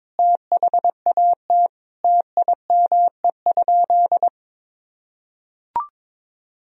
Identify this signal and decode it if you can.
Morse code, heard as a keyed tone, about 22 wpm, 700 Hz. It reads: THAT TIME?
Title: Sentences from Top 200 Words and Abbreviations 22wpm